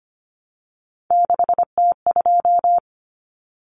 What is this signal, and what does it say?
Morse code, heard as a keyed tone, about 25 wpm, 700 Hz. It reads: 6T2
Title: Common Cut Numbers - 3-Digit 25wpm